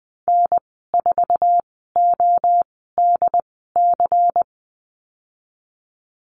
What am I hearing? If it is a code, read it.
Morse code, heard as a keyed tone, about 20 wpm, spaced as for 15 wpm, 700 Hz. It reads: N4ODC
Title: 179 Lesson 29g - Slash - Call Signs 20wpm